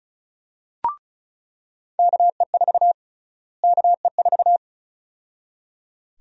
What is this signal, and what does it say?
Morse code, heard as a keyed tone, about 36 wpm, 700 Hz. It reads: KE4 KE4
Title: USA Call Sign Prefixes - Repeated 2x - 35wpm